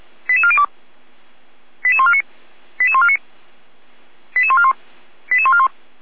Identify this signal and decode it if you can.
Five-tone selective call (ZVEI), heard as quick runs of five tones, tones of about 70 ms. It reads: 88402 88149 88149 88141 88141